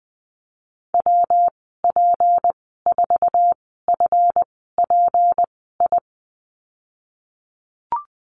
Morse code, heard as a keyed tone, about 20 wpm, spaced as for 15 wpm, 700 Hz. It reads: WP4FPI
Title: 161 Lesson 26g - Letter g - Call Signs 20wpm